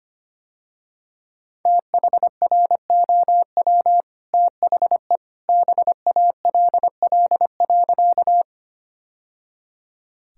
Morse code, heard as a keyed tone, about 25 wpm, 700 Hz. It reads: THROW THE BALL.
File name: Easy Sentences with Hints - 25wpm